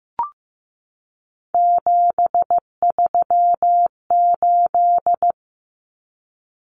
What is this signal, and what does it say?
Morse code, heard as a keyed tone, about 15 wpm, 700 Hz. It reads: 738
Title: Numbers - 3-Digit 15wpm